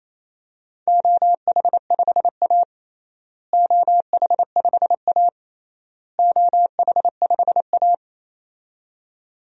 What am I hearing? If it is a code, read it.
Morse code, heard as a keyed tone, about 28 wpm, 700 Hz. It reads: OH5A OH5A OH5A
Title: DXCC-100 Call Signs - 4 Characters - Repeated 3x - 28wpm